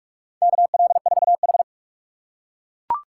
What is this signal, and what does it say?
Morse code, heard as a keyed tone, about 41 wpm, 690 Hz. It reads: KL4H
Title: USA Call Signs - 4 Characters - 3x Character Spacing - 45wpm